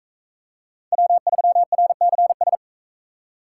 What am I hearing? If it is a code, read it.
Morse code, heard as a keyed tone, about 43 wpm, 700 Hz. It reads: W3RCS